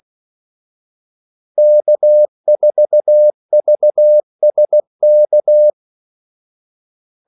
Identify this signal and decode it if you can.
Morse code, heard as a keyed tone, about 16 wpm, 600 Hz. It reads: K4VSK
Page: Kaldesignaler 16 wpm